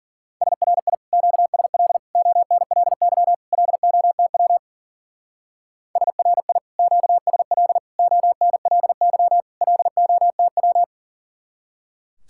Morse code, heard as a keyed tone, about 49 wpm, 700 Hz. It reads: SRI QSL ONLY LOTW SRI QSL ONLY LOTW
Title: Speed-Racing Common QSO Phrases - 40wpm